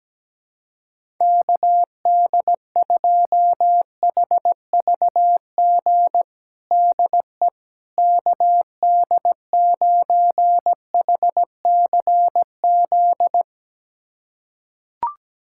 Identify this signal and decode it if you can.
Morse code, heard as a keyed tone, about 17 wpm, 700 Hz. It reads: KD2HVG DE KD9HCZ
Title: USA Callsign DE Callsign - 6 Characters - 17wpm